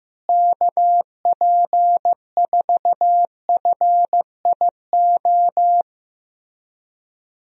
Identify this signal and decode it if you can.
Morse code, heard as a keyed tone, about 15 wpm, 700 Hz. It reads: KP4FIO
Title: USA Call Signs - 6 Characters - Repeated 3x - 15wpm